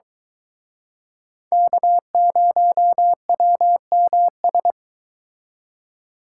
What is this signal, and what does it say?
Morse code, heard as a keyed tone, about 24 wpm, 700 Hz. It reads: K0WMS